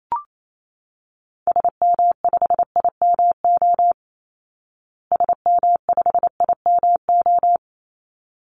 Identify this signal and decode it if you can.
Morse code, heard as a keyed tone, about 28 wpm, 700 Hz. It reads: SM5IMO SM5IMO
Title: Call Signs Repeated 3x - 28wpm